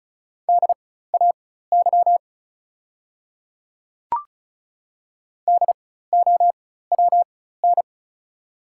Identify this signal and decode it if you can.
Morse code, heard as a keyed tone, about 36 wpm, spaced as for 16 wpm, 700 Hz. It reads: DAY DOWN
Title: 4x Character Spacing - Top 100 Words - 35wpm